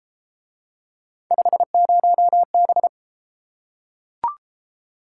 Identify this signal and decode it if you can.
Morse code, heard as a keyed tone, about 31 wpm, 700 Hz. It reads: 50B